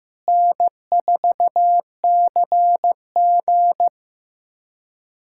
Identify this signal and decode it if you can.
Morse code, heard as a keyed tone, about 15 wpm, 700 Hz. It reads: N4CG
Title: USA Callsign DE Callsign - 4 Characters - 15wpm